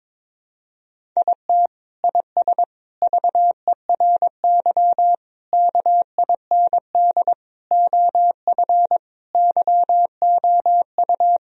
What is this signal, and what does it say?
Morse code, heard as a keyed tone, about 22 wpm, 700 Hz. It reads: IT IS VERY KIND OF YOU
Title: Sentences from Top 1000 Words - Review Part 1 - 22wpm